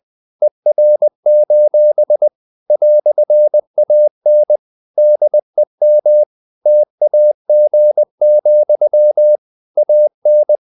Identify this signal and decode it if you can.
Morse code, heard as a keyed tone, about 20 wpm, 600 Hz. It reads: ER: "AN DEM TAG, AN
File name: dk5ke / Humor-100BpM